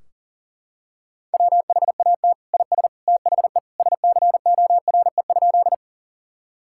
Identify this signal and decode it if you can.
Morse code, heard as a keyed tone, about 40 wpm, 700 Hz. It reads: WHAT IS THE SCORE?
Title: Sentences from Top 1600 Words 40wpm